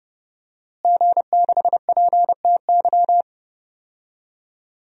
Morse code, heard as a keyed tone, about 30 wpm, 700 Hz. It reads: G6PTY
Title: DXCC-100 Call Signs - 5 Characters - Repeated 2x - 30wpm